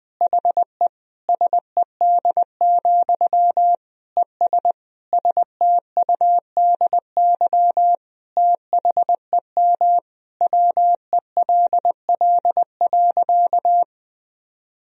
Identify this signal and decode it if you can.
Morse code, heard as a keyed tone, about 20 wpm, 700 Hz. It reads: HE SED, ES STUDY THEM WELL.
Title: Sentences from Top 300 Words and Abbreviations 20wpm